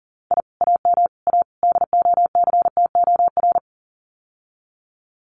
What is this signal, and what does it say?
Morse code, heard as a keyed tone, about 40 wpm, 700 Hz. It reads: I AM A DOCTOR
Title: Common QSO Phrases Encore 40wpm